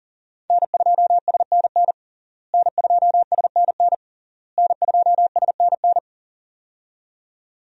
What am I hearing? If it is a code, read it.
Morse code, heard as a keyed tone, about 40 wpm, 690 Hz. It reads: N2SNN N2SNN N2SNN
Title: USA Call Signs - 5 Characters - Repeated 3x - 40wpm